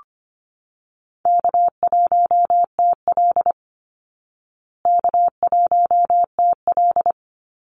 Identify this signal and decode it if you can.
Morse code, heard as a keyed tone, about 25 wpm, 700 Hz. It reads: K1TL K1TL
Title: Easy Call Signs Repeated 2x - 25wpm